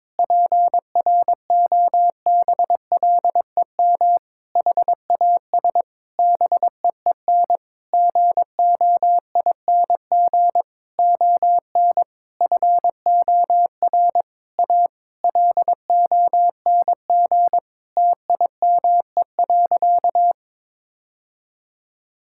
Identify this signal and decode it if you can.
Morse code, heard as a keyed tone, about 22 wpm, 700 Hz. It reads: PROBLEM HAS BEEN GOING ON FOR A LONG TIME.
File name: practice / 22wpm / Sentences with 8-Word Phrases - 22wpm